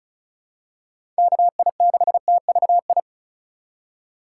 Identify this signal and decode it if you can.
Morse code, heard as a keyed tone, about 36 wpm, 700 Hz. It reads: KI6TVI
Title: USA Call Signs - 6 Characters - 2x Character Spacing - 35wpm